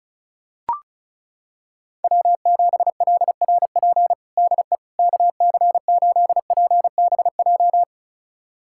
Morse code, heard as a keyed tone, about 34 wpm, 700 Hz. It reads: W7LRP DE KC8PBJ
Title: USA Callsign DE Callsign 35wpm